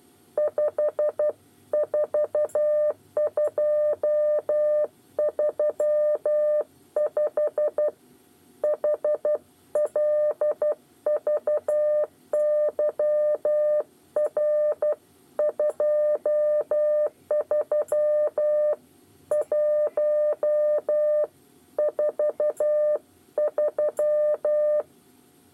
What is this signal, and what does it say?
Morse code, heard as a keyed tone, about 12 wpm, 600 Hz. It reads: 5 4235 HLVYR 23 1 43
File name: 8 bis lezione cw 2